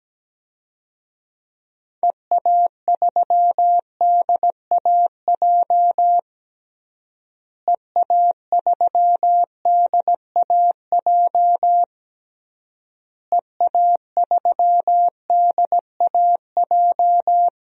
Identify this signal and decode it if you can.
Morse code, heard as a keyed tone, about 17 wpm, 700 Hz. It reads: EA3DAJ EA3DAJ EA3DAJ
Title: DXCC-100 Call Signs - 6 Characters - Repeated 3x - 17wpm